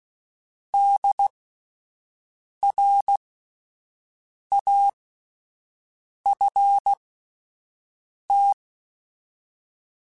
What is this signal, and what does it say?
Morse code, heard as a keyed tone, about 16 wpm, spaced as for 5 wpm, 800 Hz. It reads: DRAFT